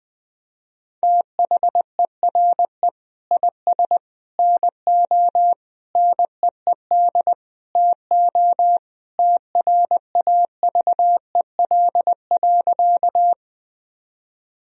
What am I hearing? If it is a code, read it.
Morse code, heard as a keyed tone, about 21 wpm, 700 Hz. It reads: THERE IS NO NEED TO TRAVEL.
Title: Sentences from Top 1000 Words - Review Part 2 - 20wpm